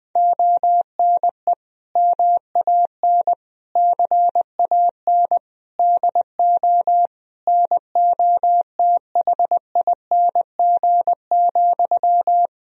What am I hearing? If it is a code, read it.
Morse code, heard as a keyed tone, about 20 wpm, 700 Hz. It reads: ONE MAN CAN DO NOTHING,
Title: Sentences from Top 1000 Words - Review Part 4 - 20wpm